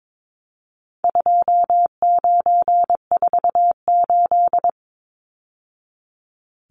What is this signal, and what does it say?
Morse code, heard as a keyed tone, about 22 wpm, 700 Hz. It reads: 2948
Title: Numbers - 4-Digit 22wpm